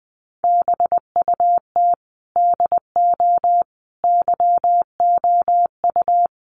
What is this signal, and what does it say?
Morse code, heard as a keyed tone, about 20 wpm, 700 Hz. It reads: BUT DO YOU
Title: Sentences from Top 600 Words 20wpm